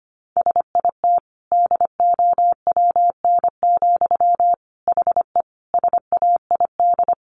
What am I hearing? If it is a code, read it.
Morse code, heard as a keyed tone, about 25 wpm, 700 Hz. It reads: SIT DOWN, HE SAID
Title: Sentences from Top 1500 Words 25wpm